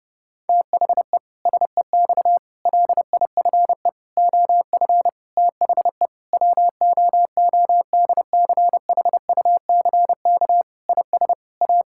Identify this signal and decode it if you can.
Morse code, heard as a keyed tone, about 30 wpm, 700 Hz. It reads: THE SEX LIFE OF THE WOODCHUCK IS A
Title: Pangram Challenge v2 - 30wpm